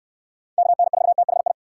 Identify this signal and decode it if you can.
Morse code, heard as a keyed tone, about 66 wpm, 700 Hz. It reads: KN4THI